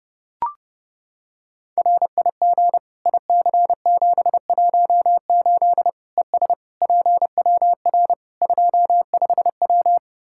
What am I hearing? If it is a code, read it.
Morse code, heard as a keyed tone, about 30 wpm, 700 Hz. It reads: RIG IC718 ES PWR 25W
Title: K1USN SST QSO Practice 30wpm